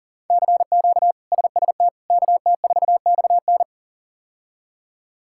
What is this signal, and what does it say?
Morse code, heard as a keyed tone, about 37 wpm, 700 Hz. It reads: CQ SST KT4XN
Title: K1USN SST QSO Practice - 40wpm